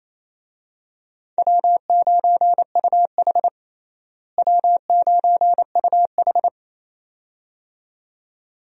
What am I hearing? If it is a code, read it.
Morse code, heard as a keyed tone, about 28 wpm, 700 Hz. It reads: W9UH W9UH
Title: USA Call Signs - 4 Characters - Repeated 2x - 28wpm